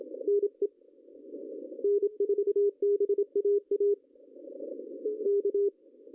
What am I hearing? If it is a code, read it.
Morse code, heard as a keyed tone, about 22 wpm, 410 Hz. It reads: NE N4BAA K
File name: CW / cqww-cw-k5tr